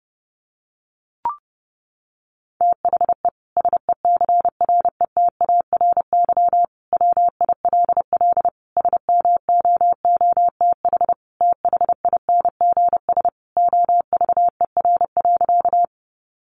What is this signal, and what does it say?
Morse code, heard as a keyed tone, about 30 wpm, 700 Hz. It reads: THE SECRETARY WILL SMOOTH THINGS OVER.
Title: Sentences from Top 1400 Words 30wpm